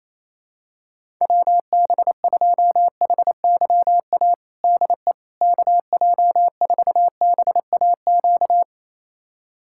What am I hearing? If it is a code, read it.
Morse code, heard as a keyed tone, about 28 wpm, 700 Hz. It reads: WB2HYA DE KJ4BAQ
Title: USA Callsign DE Callsign - 6 Characters - 28wpm